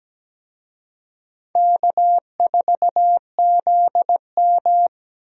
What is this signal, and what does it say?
Morse code, heard as a keyed tone, about 17 wpm, 700 Hz. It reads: K4ZM
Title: USA Call Signs - 4 Characters - 17wpm